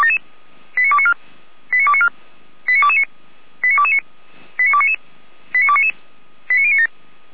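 Five-tone selective call (ZVEI), heard as quick runs of five tones, tones of about 70 ms. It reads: 89294 89294 89229 89229 89299 89299 89097
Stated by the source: ZVEI five-tone call received by SDR.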